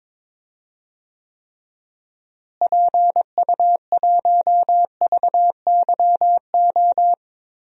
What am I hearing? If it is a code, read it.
Morse code, heard as a keyed tone, about 22 wpm, 700 Hz. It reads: PU1VYO
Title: DXCC-100 Call Signs - 6 Characters - Repeated 2x - 22wpm